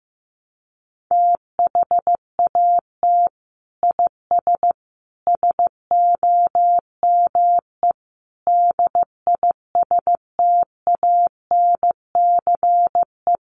Morse code, heard as a keyed tone, about 15 wpm, 700 Hz. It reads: THAT IS SOME DISTANCE
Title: Sentences from Top 1200 Words 15wpm